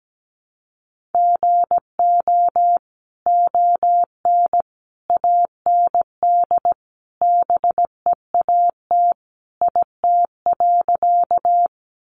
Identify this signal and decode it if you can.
Morse code, heard as a keyed tone, about 17 wpm, 700 Hz. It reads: GO ON AND BEAT IT.
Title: Sentences from Top 1300 Words 17wpm